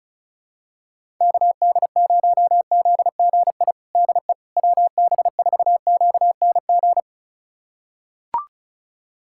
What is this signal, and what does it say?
Morse code, heard as a keyed tone, about 34 wpm, 700 Hz. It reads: KD0ZGI DE WB4QNG
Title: USA Callsign DE Callsign - 6 Characters - 35wpm